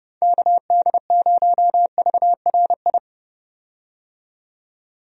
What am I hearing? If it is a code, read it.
Morse code, heard as a keyed tone, about 31 wpm, 700 Hz. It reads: KD0VRI
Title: USA Call Signs - 6 Characters - 30wpm